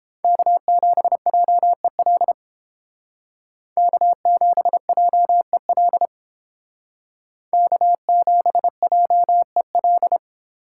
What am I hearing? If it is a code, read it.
Morse code, heard as a keyed tone, about 30 wpm, 700 Hz. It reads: K7JEL K7JEL K7JEL